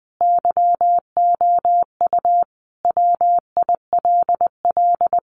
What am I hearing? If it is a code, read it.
Morse code, heard as a keyed tone, about 20 wpm, 700 Hz. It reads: YOU WILL
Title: Sentences from Top 600 Words 20wpm